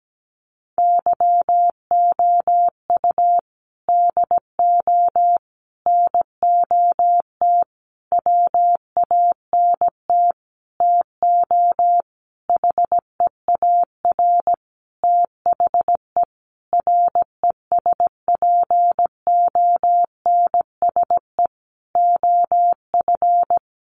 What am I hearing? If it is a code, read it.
Morse code, heard as a keyed tone, about 17 wpm, 700 Hz. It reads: YOU DO NOT WANT TO HEAR THE RESPONSE OF
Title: Sentences from Top 1400 Words 17wpm